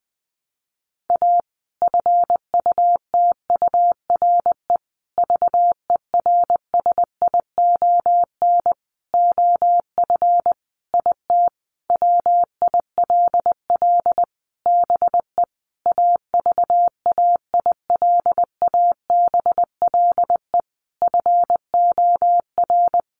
Morse code, heard as a keyed tone, about 20 wpm, 700 Hz. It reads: A FUTURE VERSION OF IT WILL BE AVAILABLE FOR
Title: Sentences from Top 1400 Words 20wpm